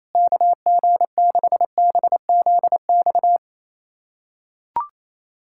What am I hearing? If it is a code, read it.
Morse code, heard as a keyed tone, about 28 wpm, 700 Hz. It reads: KG6BZX
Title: USA Callsign DE Callsign - 6 Characters - 28wpm